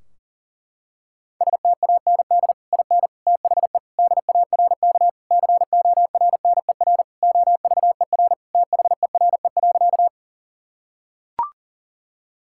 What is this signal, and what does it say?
Morse code, heard as a keyed tone, about 37 wpm, 700 Hz. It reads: STAND IN THE DARK CORNER OVER THERE.
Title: Sentences from Top 1600 Words 40wpm